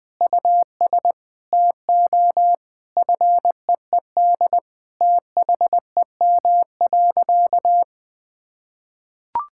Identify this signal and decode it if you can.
Morse code, heard as a keyed tone, about 21 wpm, 700 Hz. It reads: US TO FEED THEM.
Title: Sentences from Top 1600 Words 20wpm